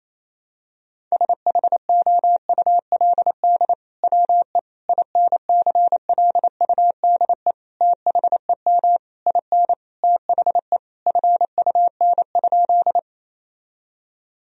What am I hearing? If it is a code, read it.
Morse code, heard as a keyed tone, about 28 wpm, 700 Hz. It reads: SHOULD WE INCLUDE THEM IN THE FUN?
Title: Sentences from Top 1000 Words - Review Part 3 - 28wpm